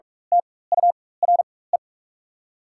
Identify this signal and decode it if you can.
Morse code, heard as a keyed tone, about 43 wpm, spaced as for 21 wpm, 690 Hz. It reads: TURE
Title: 4x Character Spacing - Top 300 Words - 45wpm